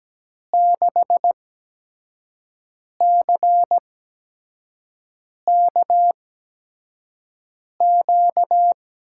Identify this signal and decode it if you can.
Morse code, heard as a keyed tone, about 17 wpm, spaced as for 5 wpm, 700 Hz. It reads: 6CKQ
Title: USA Call Signs - 6 Characters - 8x Character Spacing - 17wpm